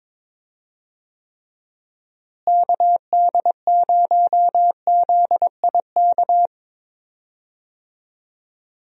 Morse code, heard as a keyed tone, about 22 wpm, 700 Hz. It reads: KD0ZIK
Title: USA Call Signs - 6 Characters - Repeated 3x - 22wpm